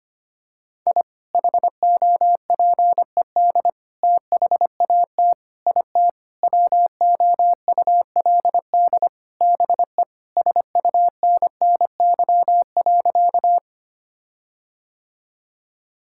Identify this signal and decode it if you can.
Morse code, heard as a keyed tone, about 25 wpm, 700 Hz. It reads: I HOPED THAT IT WOULD BE SUNNY.
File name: practice / 25wpm / Sentences with 4-Word Phrases - 25wpm